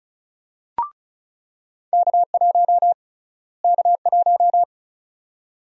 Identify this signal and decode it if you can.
Morse code, heard as a keyed tone, about 36 wpm, 700 Hz. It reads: K1 K1
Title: USA Call Sign Prefixes - Repeated 2x - 35wpm